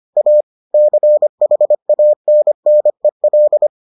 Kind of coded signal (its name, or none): Morse code